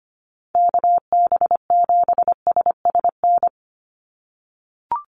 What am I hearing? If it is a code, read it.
Morse code, heard as a keyed tone, about 25 wpm, 690 Hz. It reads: KB7SSN
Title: USA Call Signs 25wpm